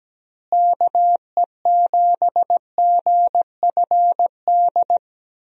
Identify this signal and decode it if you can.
Morse code, heard as a keyed tone, about 17 wpm, 700 Hz. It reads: KE7GFD
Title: USA Call Signs - 6 Characters - Repeated 3x - 17wpm